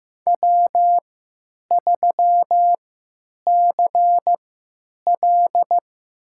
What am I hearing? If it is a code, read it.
Morse code, heard as a keyed tone, about 15 wpm, spaced as for 8 wpm, 700 Hz. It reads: W3CL
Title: USA Call Signs - 4 Characters - 3x Character Spacing - 15wpm